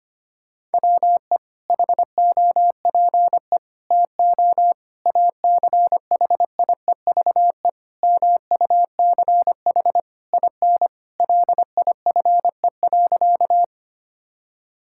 Morse code, heard as a keyed tone, about 25 wpm, 700 Hz. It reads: WE HOPE TO ACHIEVE MUCH IN LIFE.
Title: Sentences from Top 1000 Words - Review Part 6 - 25wpm